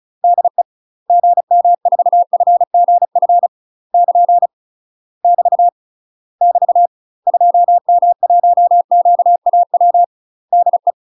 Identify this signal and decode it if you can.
Morse code, heard as a keyed tone, about 34 wpm, 700 Hz. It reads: DE GM4FGF <KN> <BT> <BT> 2M1QAW DE
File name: UK MASTER QSOs TRAINING / 35WPM / UK MASTER QSOs 0043